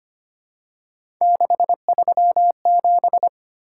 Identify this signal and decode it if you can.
Morse code, heard as a keyed tone, about 25 wpm, 700 Hz. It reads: 637
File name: Three Letter-Number Combinations - 25wpm